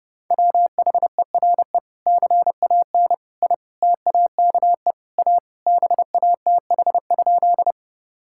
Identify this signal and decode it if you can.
Morse code, heard as a keyed tone, about 30 wpm, 700 Hz. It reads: WHERE CAN I TAKE A BATH?
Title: Sentences from Top 1600 Words 30wpm